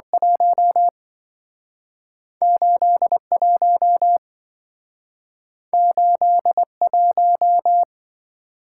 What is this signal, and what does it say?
Morse code, heard as a keyed tone, about 27 wpm, 700 Hz. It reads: 1 81 81